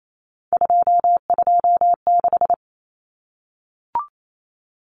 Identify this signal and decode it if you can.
Morse code, heard as a keyed tone, about 28 wpm, 700 Hz. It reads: 226
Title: Numbers - 3-Digit 28wpm